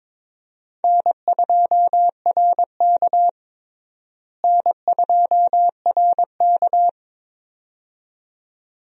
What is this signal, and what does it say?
Morse code, heard as a keyed tone, about 22 wpm, 700 Hz. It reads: N2RK N2RK
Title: USA Call Signs - 4 Characters - Repeated 2x - 22wpm